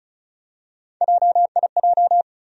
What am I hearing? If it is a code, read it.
Morse code, heard as a keyed tone, about 36 wpm, 700 Hz. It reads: JIJ
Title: USA Call Sign Postfixes - 3 Letters 35wpm